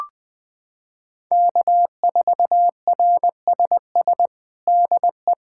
Morse code, heard as a keyed tone, about 21 wpm, 700 Hz. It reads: K4RSS DE
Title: USA Callsign DE Callsign 20wpm